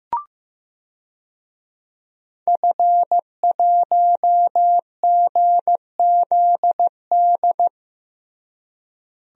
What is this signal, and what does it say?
Morse code, heard as a keyed tone, about 16 wpm, 700 Hz. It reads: F1GZD
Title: DXCC-100 Call Signs - 5 Characters - Repeated 2x - 15wpm